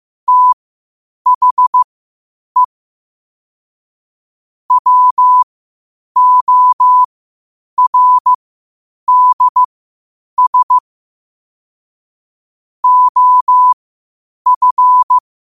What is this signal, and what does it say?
Morse code, heard as a keyed tone, about 15 wpm, spaced as for 8 wpm, 1000 Hz. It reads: THE WORDS OF
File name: qotdfiles / 8wpm